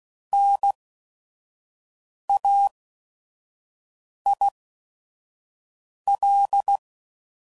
Morse code, heard as a keyed tone, about 16 wpm, spaced as for 5 wpm, 800 Hz. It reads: NAIL
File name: Prefixes / By First Letter / Ns